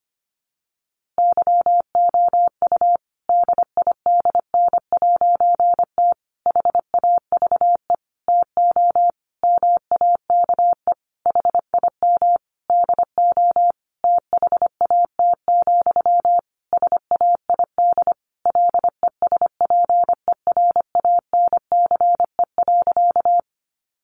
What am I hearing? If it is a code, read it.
Morse code, heard as a keyed tone, about 25 wpm, 700 Hz. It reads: YOU DIDN'T HAVE TO MAKE HIM DO THAT, SAID LESPERANCE.
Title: A Sound Of Thunder 25wpm